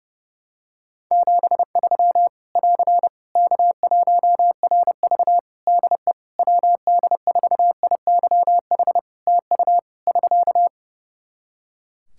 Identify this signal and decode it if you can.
Morse code, heard as a keyed tone, about 30 wpm, 700 Hz. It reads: 73 <AR> K1RV DE WD4IYH TU <SK>
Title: Practice QSO - Exchange 4 - Ending - 30wpm